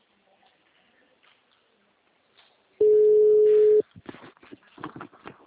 Ringback tone (Europe, 425 Hz).